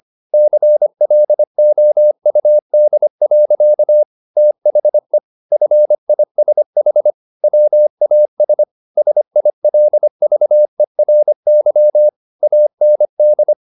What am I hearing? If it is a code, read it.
Morse code, heard as a keyed tone, about 25 wpm, 600 Hz. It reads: CLOUD. THE FISH WAS SILVERY AND